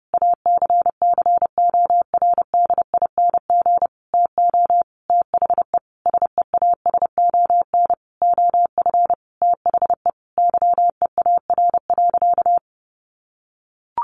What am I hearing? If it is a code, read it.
Morse code, heard as a keyed tone, about 30 wpm, 700 Hz. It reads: ACCORDING TO THE SEASON OF THE YEAR.
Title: Sentences from Top 1400 Words 30wpm